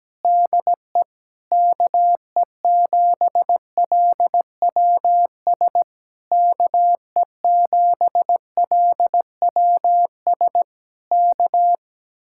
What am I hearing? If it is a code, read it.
Morse code, heard as a keyed tone, about 17 wpm, 700 Hz. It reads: DE KE7LWS KE7LWS K
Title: Speed-Racing Common QSO Phrases - 17wpm